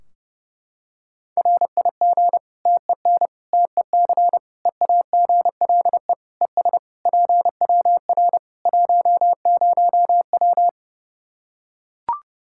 Morse code, heard as a keyed tone, about 32 wpm, 700 Hz. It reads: RIG TEN TEC EAGLE ES PWR 10W
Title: K1USN SST QSO Practice 30wpm